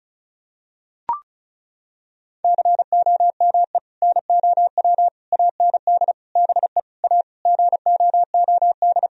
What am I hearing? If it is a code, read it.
Morse code, heard as a keyed tone, about 36 wpm, 700 Hz. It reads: COME NOW AND BE A GOOD
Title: Sentences from Top 1000 Words - Review Part 4 - 35wpm